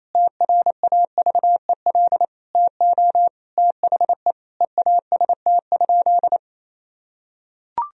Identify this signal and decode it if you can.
Morse code, heard as a keyed tone, about 28 wpm, 700 Hz. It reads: TRAVEL TO THE EAST?
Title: Sentences from Top 1000 Words - Review Part 3 - 28wpm